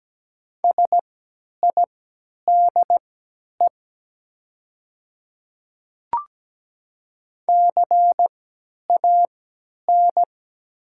Morse code, heard as a keyed tone, about 17 wpm, spaced as for 10 wpm, 690 Hz. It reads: SIDE CAN
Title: 3x Character Spacing - Top 100 Words - 17wpm